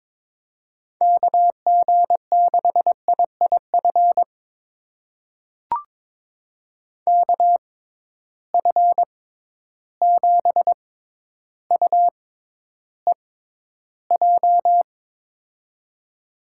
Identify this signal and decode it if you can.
Morse code, heard as a keyed tone, about 22 wpm, spaced as for 11 wpm, 700 Hz. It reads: KG6IIF KF7UEJ
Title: USA Call Signs - 6 Characters - 6x Character Spacing - 22wpm